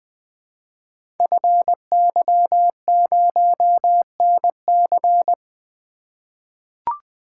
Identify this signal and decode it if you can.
Morse code, heard as a keyed tone, about 20 wpm, 700 Hz. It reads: FY0NC
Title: Code Groups - 5 Letter:Numbers - 20wpm